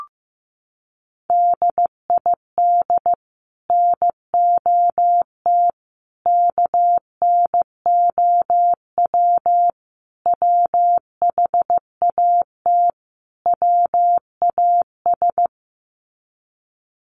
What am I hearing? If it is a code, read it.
Morse code, heard as a keyed tone, about 15 wpm, 700 Hz. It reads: DID NOT KNOW WHAT WAS
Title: Sets of 5 Words Encore2 15wpm